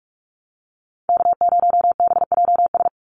Morse code, heard as a keyed tone, about 44 wpm, 700 Hz. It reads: K0BJS
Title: USA Call Signs 45wpm